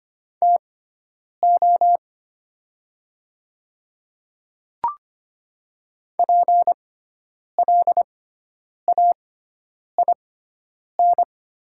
Morse code, heard as a keyed tone, about 25 wpm, spaced as for 9 wpm, 700 Hz. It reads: TO PLAIN
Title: 6x Character Spacing - Top 500 Words - 25wpm